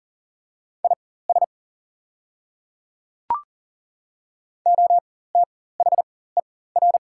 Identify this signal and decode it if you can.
Morse code, heard as a keyed tone, about 40 wpm, spaced as for 19 wpm, 690 Hz. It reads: IS OTHER
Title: 4x Character Spacing - Top 100 Words - 40wpm